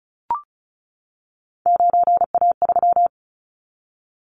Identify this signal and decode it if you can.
Morse code, heard as a keyed tone, about 36 wpm, 700 Hz. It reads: 9A3
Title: Callsign Prefixes for Top 50 DXCC - 35wpm